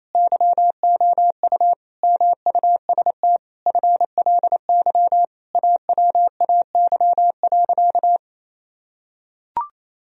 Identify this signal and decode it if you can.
Morse code, heard as a keyed tone, about 28 wpm, 700 Hz. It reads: YOU MUST FLY AWAY.
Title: Sentences from Top 1000 Words - Review Part 2 - 28wpm